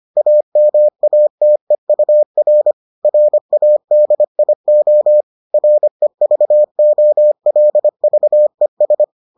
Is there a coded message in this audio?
Yes: Morse code from 0.2 to 9.0 s, heard as a keyed tone, about 25 wpm, 600 Hz. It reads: AMATEUR RADIO REVOLVES